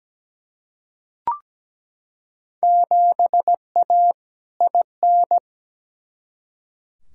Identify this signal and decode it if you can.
Morse code, heard as a keyed tone, about 17 wpm, 700 Hz. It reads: 7A IN
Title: Field Day Exchanges Encore - 17wpm